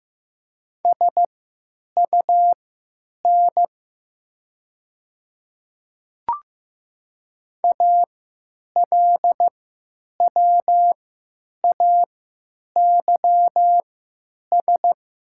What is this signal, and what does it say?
Morse code, heard as a keyed tone, about 15 wpm, spaced as for 9 wpm, 700 Hz. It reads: SUN ALWAYS
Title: 3x Character Spacing - Top 300 Words - 15wpm